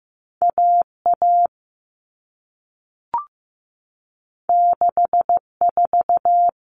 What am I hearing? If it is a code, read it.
Morse code, heard as a keyed tone, about 15 wpm, 700 Hz. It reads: AA 64
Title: All Cut Numbers - 2-Digit 15wpm